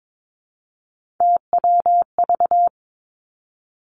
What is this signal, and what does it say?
Morse code, heard as a keyed tone, about 22 wpm, 700 Hz. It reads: TWV
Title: Three Challenging Letter-Number Combinations 22wpm